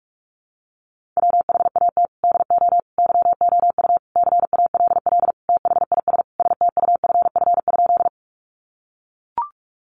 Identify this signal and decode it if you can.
Morse code, heard as a keyed tone, about 44 wpm, 700 Hz. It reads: WHAT DO YOU CALL THIS STUFF?
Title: Sentences from Top 1400 Words 45wpm